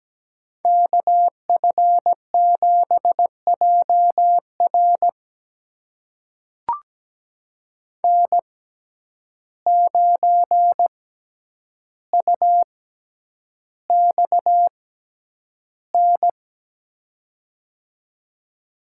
Morse code, heard as a keyed tone, about 17 wpm, spaced as for 9 wpm, 700 Hz. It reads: KF7JR N9UXN
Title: USA Call Signs - 5 Characters - 6x Character Spacing - 17wpm